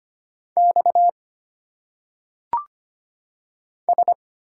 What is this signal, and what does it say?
Morse code, heard as a keyed tone, about 25 wpm, 700 Hz. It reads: X S
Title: Single Letters - Phonetic Pronunciation 25wpm